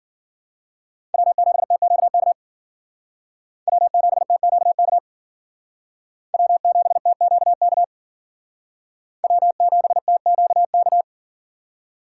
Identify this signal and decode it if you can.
Morse code, heard as a keyed tone, about 52 wpm, 700 Hz. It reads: W7TQK W7TQK W7TQK W7TQK